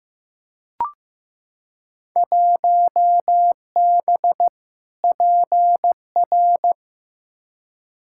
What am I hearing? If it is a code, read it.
Morse code, heard as a keyed tone, about 15 wpm, 700 Hz. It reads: 1B PR
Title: Field Day Exchanges Encore - 15wpm